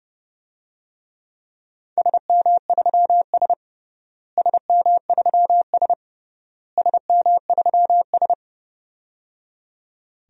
Morse code, heard as a keyed tone, about 30 wpm, 700 Hz. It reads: SM3S SM3S SM3S
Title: DXCC-100 Call Signs - 4 Characters - Repeated 3x - 30wpm